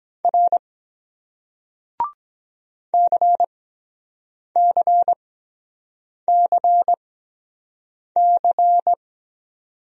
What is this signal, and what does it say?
Morse code, heard as a keyed tone, about 23 wpm, spaced as for 7 wpm, 700 Hz. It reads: R CCCC